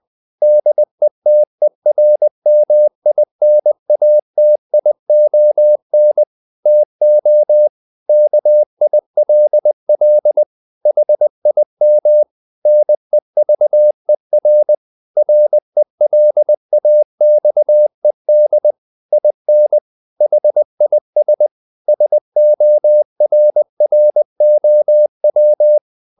Morse code, heard as a keyed tone, about 20 wpm, 600 Hz. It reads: DETERMINATION TO KILL HIM NEVER RELAXED IN HIS SORROW
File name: the old man and the sea part 2 20wpm